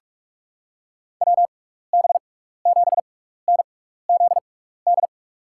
Morse code, heard as a keyed tone, about 44 wpm, spaced as for 15 wpm, 700 Hz. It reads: WB7NZD
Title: USA Call Signs - 6 Characters - 6x Character Spacing - 45wpm